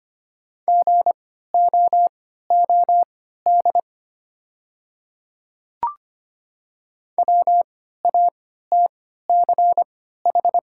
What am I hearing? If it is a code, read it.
Morse code, heard as a keyed tone, about 25 wpm, spaced as for 14 wpm, 700 Hz. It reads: GOOD WATCH
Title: 3x Character Spacing - Top 300 Words - 25wpm